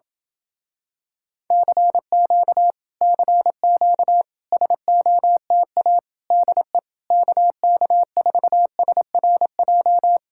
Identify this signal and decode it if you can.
Morse code, heard as a keyed tone, about 27 wpm, 700 Hz. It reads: CQ CQ SOTA DE KK4SRJ